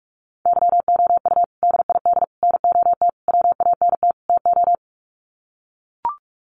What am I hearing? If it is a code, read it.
Morse code, heard as a keyed tone, about 44 wpm, 700 Hz. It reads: YOU DID NOT WANT TO
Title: Sets of 5 Words Encore2 45wpm